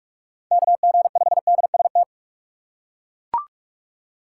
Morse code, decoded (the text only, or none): KG5DST